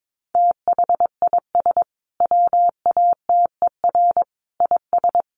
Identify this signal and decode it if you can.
Morse code, heard as a keyed tone, about 22 wpm, 700 Hz. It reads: THIS WATER IS
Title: Sentences from Top 1300 Words 22wpm